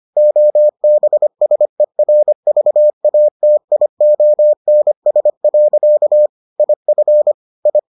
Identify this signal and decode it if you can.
Morse code, heard as a keyed tone, about 25 wpm, 600 Hz. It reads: OBSERVATIONS. IF I